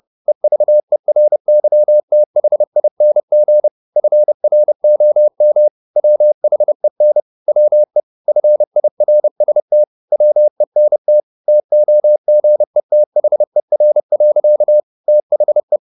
Morse code, heard as a keyed tone, about 30 wpm, 600 Hz. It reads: EVERYTHING FROM WHEN WE FIRST WENT TOGETHER. THE
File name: the old man and the sea / the old man and the sea 1 30wpm